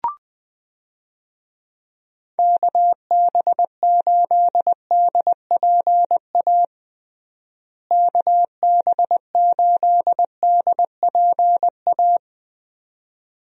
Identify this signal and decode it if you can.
Morse code, heard as a keyed tone, about 20 wpm, 700 Hz. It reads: KB8DPA KB8DPA
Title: USA Call Signs - 6 Characters - Repeated 2x - 20wpm